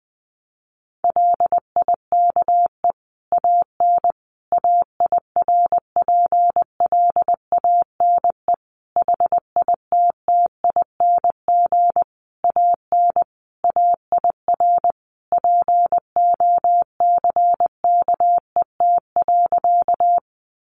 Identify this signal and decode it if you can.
Morse code, heard as a keyed tone, about 20 wpm, 700 Hz. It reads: LIKE AN AIRPLANE HITTING AN AIR POCKET.
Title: A Sound Of Thunder 20wpm